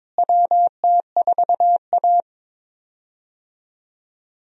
Morse code, heard as a keyed tone, about 22 wpm, 700 Hz. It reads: WT4A
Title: USA Call Signs - 4 Characters - Repeated 3x - 22wpm